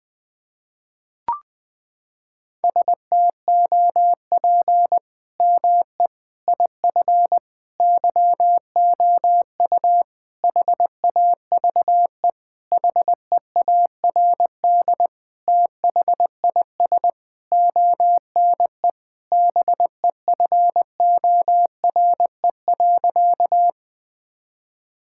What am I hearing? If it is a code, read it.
Morse code, heard as a keyed tone, about 20 wpm, 700 Hz. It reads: STOP ME IF YOU HAVE HEARD THIS ONE BEFORE.
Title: Sentences with 8-Word Phrases - 20wpm